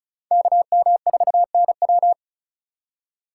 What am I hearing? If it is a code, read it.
Morse code, heard as a keyed tone, about 36 wpm, 700 Hz. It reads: KM4NW
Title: USA Call Signs - 4x Character Spacing - 35wpm